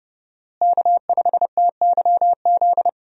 Morse code, heard as a keyed tone, about 30 wpm, 700 Hz. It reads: K5TYZ
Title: USA Call Signs - 5 Characters - Repeated 3x - 30wpm